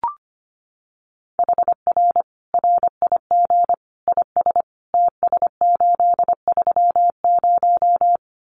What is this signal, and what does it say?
Morse code, heard as a keyed tone, about 25 wpm, 700 Hz. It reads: HR RIG IS TS830
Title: Common QSO Phrases 25wpm